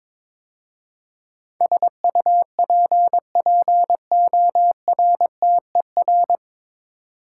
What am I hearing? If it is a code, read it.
Morse code, heard as a keyed tone, about 22 wpm, 700 Hz. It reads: SUPPORTER
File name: 9 Letter Words - First 3 Letters Given - 22wpm